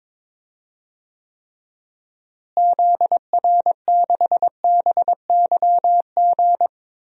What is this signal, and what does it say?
Morse code, heard as a keyed tone, about 22 wpm, 700 Hz. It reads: ZR6BYG
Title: DXCC-100 Call Signs - 6 Characters - Repeated 2x - 22wpm